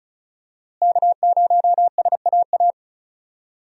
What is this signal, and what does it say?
Morse code, heard as a keyed tone, about 36 wpm, 700 Hz. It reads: K0SAA
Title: USA Call Signs - 5 Characters - Repeated 3x - 35wpm